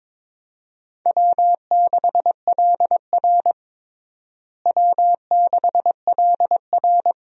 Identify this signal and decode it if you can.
Morse code, heard as a keyed tone, about 22 wpm, 700 Hz. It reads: W6LR W6LR
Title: USA Call Signs - 4 Characters - Repeated 3x - 22wpm